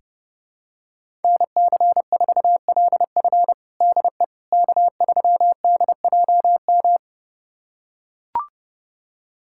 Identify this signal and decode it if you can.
Morse code, heard as a keyed tone, about 30 wpm, 700 Hz. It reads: NC4LF DE K3DJM
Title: USA Callsign DE Callsign - 5 Characters - 30wpm